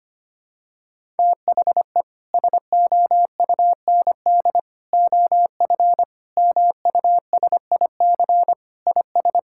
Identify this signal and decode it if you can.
Morse code, heard as a keyed tone, about 25 wpm, 700 Hz. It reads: THE SOUND OF MUSIC IS
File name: practice / 25wpm / Sentences with 3-Word Phrases - 25wpm